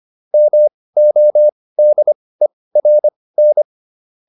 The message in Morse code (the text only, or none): MODERN